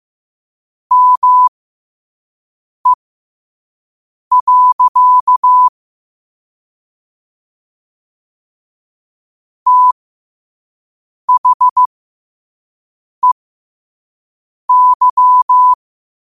Morse code, heard as a keyed tone, about 15 wpm, spaced as for 5 wpm, 1000 Hz. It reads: ME. THEY
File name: qotdfiles / 5wpm